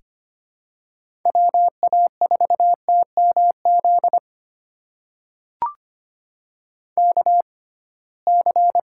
Morse code, heard as a keyed tone, about 25 wpm, spaced as for 19 wpm, 700 Hz. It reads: WA4TMZ KC
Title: USA Call Signs - 6 Characters - 6x Character Spacing - 25wpm